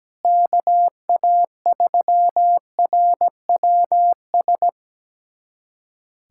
Morse code, heard as a keyed tone, about 17 wpm, 700 Hz. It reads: KA3RWS
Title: USA Callsign DE Callsign - 6 Characters - 17wpm